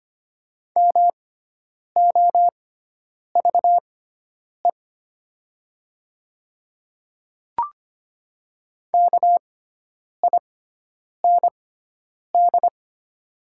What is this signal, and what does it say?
Morse code, heard as a keyed tone, about 25 wpm, spaced as for 9 wpm, 700 Hz. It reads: MOVE KIND
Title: 6x Character Spacing - Top 200 Words - 25wpm